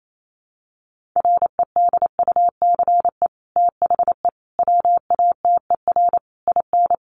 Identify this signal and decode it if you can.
Morse code, heard as a keyed tone, about 28 wpm, 700 Hz. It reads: REDUCE THE WATER IN
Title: Sentences from Top 1100 Words 28wpm